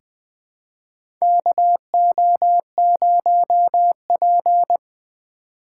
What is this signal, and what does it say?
Morse code, heard as a keyed tone, about 20 wpm, 700 Hz. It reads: KO0P
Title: USA Call Signs - 4 Characters - 6x Character Spacing - 20wpm